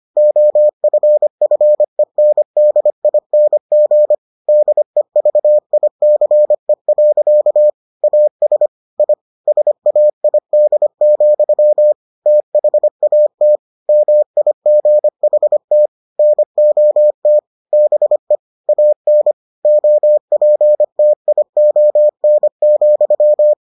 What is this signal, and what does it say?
Morse code, heard as a keyed tone, about 25 wpm, 600 Hz. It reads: OFFENDING DEVICE. AS I SAID, THAT MIGHT NOT BE AN OPTION,